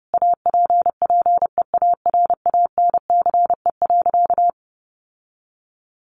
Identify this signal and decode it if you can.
Morse code, heard as a keyed tone, about 30 wpm, 700 Hz. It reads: APPEARANCE.
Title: Sentences from Top 1100 Words 30wpm